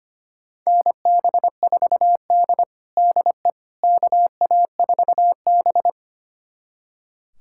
Morse code, heard as a keyed tone, about 25 wpm, 700 Hz. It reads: NB4D DE KA4B
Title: USA Callsign DE Callsign - 4 Characters - 25wpm